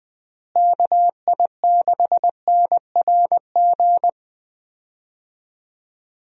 Morse code, heard as a keyed tone, about 20 wpm, 700 Hz. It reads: KI6NRG
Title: USA Call Signs - 6 Characters - Repeated 3x - 20wpm